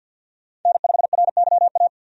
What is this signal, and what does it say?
Morse code, heard as a keyed tone, about 53 wpm, 700 Hz. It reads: N5RYA
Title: USA Call Signs - 5 Characters - 50wpm